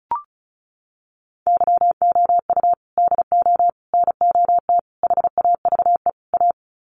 Morse code, heard as a keyed tone, about 34 wpm, 700 Hz. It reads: YOU DO NOT HAVE A
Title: Sets of 5 Words 35wpm